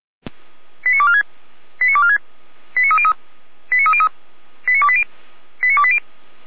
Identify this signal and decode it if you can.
Five-tone selective call (ZVEI), heard as quick runs of five tones, tones of about 70 ms. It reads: 89246 89246 89303 89303 89280 89280